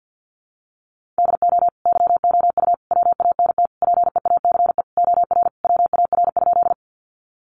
Morse code, heard as a keyed tone, about 50 wpm, 700 Hz. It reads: DO YOU WANT PEACE OR WAR?
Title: Sentences from Top 1500 Words 50wpm